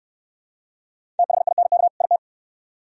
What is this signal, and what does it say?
Morse code, heard as a keyed tone, about 62 wpm, 700 Hz. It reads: THINK IT